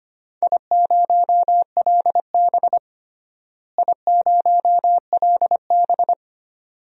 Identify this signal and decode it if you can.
Morse code, heard as a keyed tone, about 25 wpm, 700 Hz. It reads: I0LB I0LB
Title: DXCC-100 Call Signs - 4 Characters - Repeated 2x - 25wpm